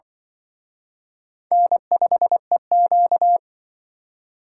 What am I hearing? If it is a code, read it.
Morse code, heard as a keyed tone, about 25 wpm, 700 Hz. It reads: N5EQ